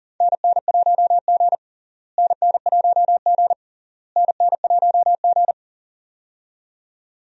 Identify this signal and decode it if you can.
Morse code, heard as a keyed tone, about 40 wpm, 700 Hz. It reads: NN1G NN1G NN1G
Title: USA Call Signs - 4 Characters - Repeated 3x - 40wpm